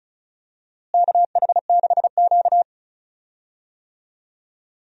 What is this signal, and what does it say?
Morse code, heard as a keyed tone, about 36 wpm, 700 Hz. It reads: KH6Q
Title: DXCC-100 Call Signs - 4 Characters - Repeated 3x - 35wpm